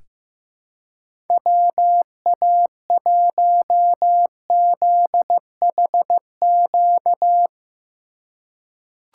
Morse code, heard as a keyed tone, about 15 wpm, 700 Hz. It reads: WA1ZHQ
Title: USA Call Signs - 6 Characters - 4x Character Spacing - 15wpm